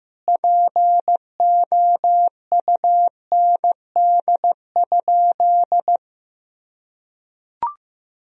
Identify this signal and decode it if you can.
Morse code, heard as a keyed tone, about 15 wpm, 700 Hz. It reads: POUND?
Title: Sentences from Top 1000 Words - Review Part 2 - 15wpm